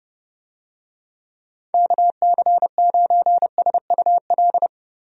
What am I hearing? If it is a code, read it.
Morse code, heard as a keyed tone, about 32 wpm, 700 Hz. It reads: KC9SUL
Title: USA Call Signs - 6 Characters - Repeated 2x - 30wpm